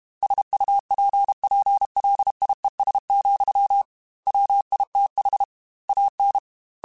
Morse code, heard as a keyed tone, about 32 wpm, 800 Hz. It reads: SUPPLIES, WITH AN
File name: Technical Subjects / Technical Subjects-mp3s / Sw Pwr Sup @30